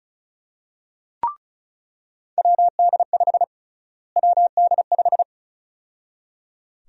Morse code, heard as a keyed tone, about 36 wpm, 700 Hz. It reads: WD5 WD5
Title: USA Call Sign Prefixes - Repeated 2x - 35wpm